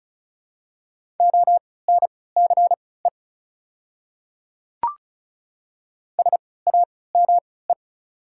Morse code, heard as a keyed tone, about 34 wpm, spaced as for 20 wpm, 690 Hz. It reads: ONCE SAME
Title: 3x Character Spacing - Top 500 Words - 35wpm